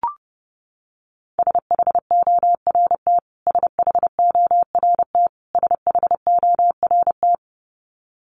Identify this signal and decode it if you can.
Morse code, heard as a keyed tone, about 30 wpm, 700 Hz. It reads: SHORT SHORT SHORT
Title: Top 500 Words Repeated 30wpm